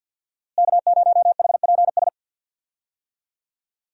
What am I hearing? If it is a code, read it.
Morse code, heard as a keyed tone, about 50 wpm, 700 Hz. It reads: K0HPS
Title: USA Call Signs - 5 Characters - Repeated 3x - 50wpm